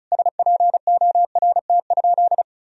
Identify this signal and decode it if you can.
Morse code, heard as a keyed tone, about 34 wpm, 700 Hz. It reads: SPORT?
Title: Sentences from Top 1000 Words - Review Part 4 - 35wpm